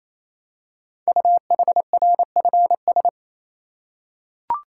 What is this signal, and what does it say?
Morse code, heard as a keyed tone, about 27 wpm, 700 Hz. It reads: UHRFS
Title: Code Groups - 5 Letters - 28wpm